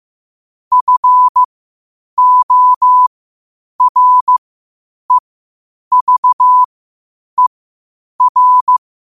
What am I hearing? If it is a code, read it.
Morse code, heard as a keyed tone, about 15 wpm, spaced as for 8 wpm, 1000 Hz. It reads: FOREVER
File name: qotdfiles / 8wpm